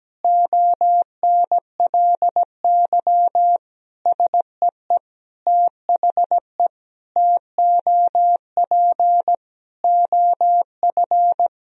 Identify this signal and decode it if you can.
Morse code, heard as a keyed tone, about 17 wpm, 700 Hz. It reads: ONLY SEE THE TOP OF
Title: Sentences from Top 1000 Words - Review Part 2 - 17wpm